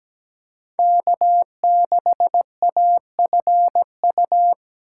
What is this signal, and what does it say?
Morse code, heard as a keyed tone, about 17 wpm, 700 Hz. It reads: K6AFU
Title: USA Call Signs - 2x Character Spacing - 17wpm